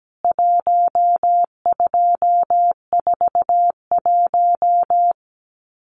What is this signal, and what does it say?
Morse code, heard as a keyed tone, about 17 wpm, 700 Hz. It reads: 1241
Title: CWT Exchange 17wpm